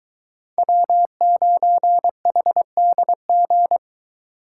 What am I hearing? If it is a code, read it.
Morse code, heard as a keyed tone, about 22 wpm, 700 Hz. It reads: W9HDG